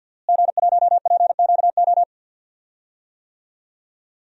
Morse code, heard as a keyed tone, about 51 wpm, 700 Hz. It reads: G1PQO
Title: DXCC-100 Call Signs - 5 Characters - Repeated 3x - 50wpm